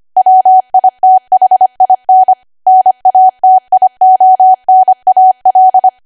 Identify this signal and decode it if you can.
Morse code, heard as a keyed tone, about 25 wpm, 750 Hz. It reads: WITHIN NATIONAL
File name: Archive / 25 WPM / 25WPM